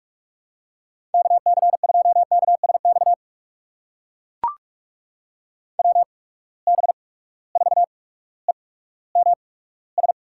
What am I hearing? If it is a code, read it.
Morse code, heard as a keyed tone, about 42 wpm, spaced as for 19 wpm, 700 Hz. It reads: KC2KSX WB4EMS
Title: USA Call Signs - 6 Characters - 8x Character Spacing - 45wpm